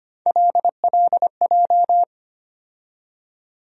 Morse code, heard as a keyed tone, about 25 wpm, 700 Hz. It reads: LLJ
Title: USA Call Sign Postfixes - 3 Letters 25wpm